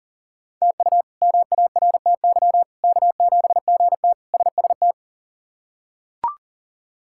Morse code, heard as a keyed tone, about 37 wpm, 700 Hz. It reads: TU MARTY K7GT SST
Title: K1USN SST QSO Practice - 40wpm